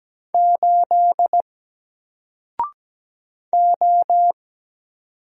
Morse code, heard as a keyed tone, about 17 wpm, 700 Hz. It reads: 8 O
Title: Single Letter-Number 17wpm